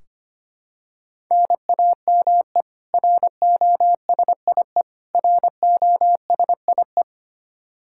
Morse code, heard as a keyed tone, about 25 wpm, 700 Hz. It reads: NAME ROSIE ROSIE
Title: K1USN SST QSO Practice 25wpm